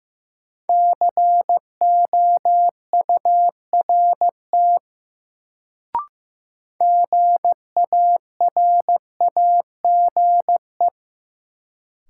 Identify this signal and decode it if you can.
Morse code, heard as a keyed tone, about 16 wpm, 700 Hz. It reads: COURT GARAGE
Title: Sentences from Top 1600 Words - New Word Drill - 15wpm